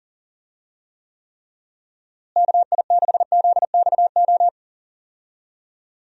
Morse code, heard as a keyed tone, about 40 wpm, 700 Hz. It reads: KI6ZXO
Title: USA Call Signs - 6 Characters - Repeated 3x - 40wpm